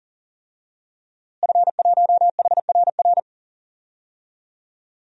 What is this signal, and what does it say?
Morse code, heard as a keyed tone, about 37 wpm, 700 Hz. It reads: F1HRR
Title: DXCC-100 Call Signs - 5 Characters - Repeated 3x - 40wpm